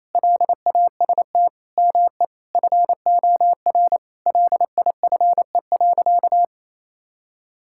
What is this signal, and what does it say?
Morse code, heard as a keyed tone, about 28 wpm, 700 Hz. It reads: LAST ME FOR LIFE.
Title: Sentences from Top 1000 Words - Review Part 6 - 28wpm